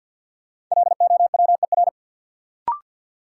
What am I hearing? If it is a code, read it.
Morse code, heard as a keyed tone, about 53 wpm, 700 Hz. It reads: LOWER
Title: Sentences from Top 700 Words - New Word Drill - 50wpm